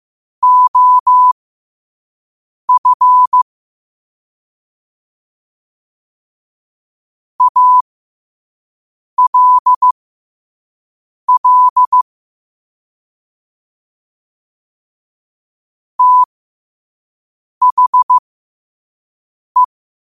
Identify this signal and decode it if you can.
Morse code, heard as a keyed tone, about 15 wpm, spaced as for 5 wpm, 1000 Hz. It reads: OF ALL THE